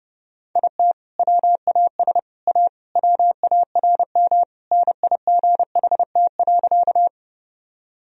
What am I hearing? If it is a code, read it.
Morse code, heard as a keyed tone, about 30 wpm, 700 Hz. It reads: IT WAS A WARM NIGHT.
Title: Sentences from Top 1000 Words - Review Part 2 - 30wpm